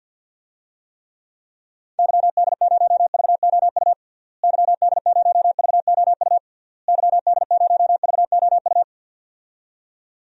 Morse code, heard as a keyed tone, about 50 wpm, 700 Hz. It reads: YD0VOU YD0VOU YD0VOU
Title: DXCC-100 Call Signs - 6 Characters - Repeated 3x - 50wpm